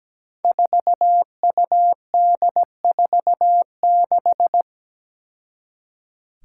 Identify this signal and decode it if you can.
Morse code, heard as a keyed tone, about 17 wpm, 700 Hz. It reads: 4UD46
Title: Five Challenging Letter-Number Combinations 17wpm